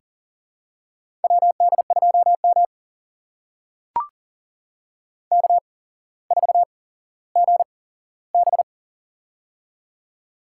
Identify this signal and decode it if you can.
Morse code, heard as a keyed tone, about 40 wpm, spaced as for 17 wpm, 700 Hz. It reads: WD2M K4GB
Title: USA Call Signs - 4 Characters - 8x Character Spacing - 40wpm